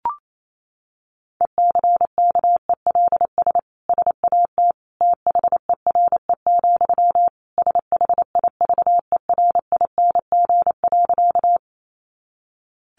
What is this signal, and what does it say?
Morse code, heard as a keyed tone, about 28 wpm, 700 Hz. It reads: ECKELS SAT THERE, SHIVERING.
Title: A Sound Of Thunder 28wpm